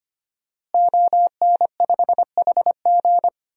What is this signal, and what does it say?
Morse code, heard as a keyed tone, about 25 wpm, 700 Hz. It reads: ON5HG
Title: DXCC-100 Call Signs - 5 Characters - Repeated 3x - 25wpm